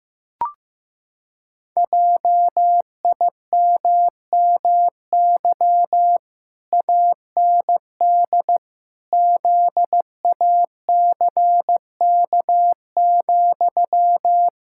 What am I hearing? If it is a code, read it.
Morse code, heard as a keyed tone, about 15 wpm, 700 Hz. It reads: JIMMY AND ZACK,
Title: Pangram Challenge v2 - 15wpm